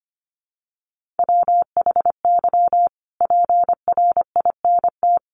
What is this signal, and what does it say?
Morse code, heard as a keyed tone, about 25 wpm, 700 Hz. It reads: WHY PRINT
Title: Sentences from Top 1400 Words 25wpm